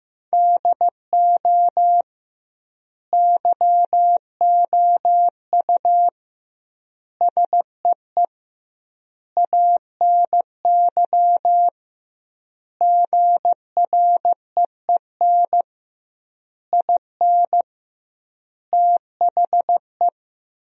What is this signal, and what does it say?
Morse code, heard as a keyed tone, about 15 wpm, 700 Hz. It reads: DO YOU SEE ANY GREEN IN THE
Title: 2x Word-Spacing - Sentences from Top 500 Words - 15wpm